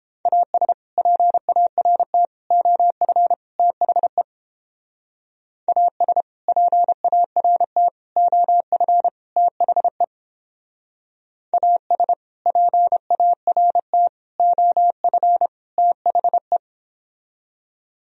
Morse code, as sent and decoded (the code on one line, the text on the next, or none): .- ... / .--. .- .-. - / --- ..-. / - .... / .- ... / .--. .- .-. - / --- ..-. / - .... / .- ... / .--. .- .-. - / --- ..-. / - .... .
AS PART OF THE AS PART OF THE AS PART OF THE